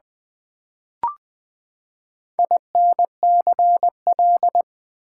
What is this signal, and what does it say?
Morse code, heard as a keyed tone, about 20 wpm, 700 Hz. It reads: INCL
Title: Build A Word - 4 Letters At A Time - 8-Letter Words - 20wpm